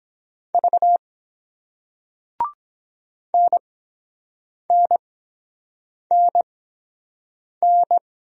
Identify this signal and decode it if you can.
Morse code, heard as a keyed tone, about 24 wpm, spaced as for 7 wpm, 700 Hz. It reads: V NNNN